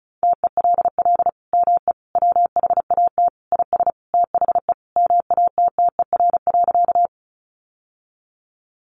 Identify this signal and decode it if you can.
Morse code, heard as a keyed tone, about 34 wpm, 700 Hz. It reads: TELL ME WHAT IS THE MATTER.
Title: Sentences from Top 700 Words 35wpm